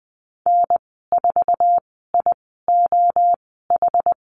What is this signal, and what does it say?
Morse code, heard as a keyed tone, about 20 wpm, spaced as for 15 wpm, 700 Hz. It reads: N4IOH
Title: 082 Lesson 14g - Number 2 - Call Signs 20wpm